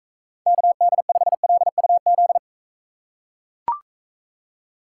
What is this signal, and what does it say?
Morse code, heard as a keyed tone, about 41 wpm, 700 Hz. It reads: KD5LUZ